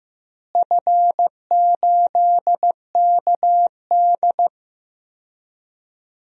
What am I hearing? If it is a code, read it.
Morse code, heard as a keyed tone, about 15 wpm, 700 Hz. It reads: F8KD
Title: DXCC-100 Call Signs - 4 Characters - Repeated 2x - 15wpm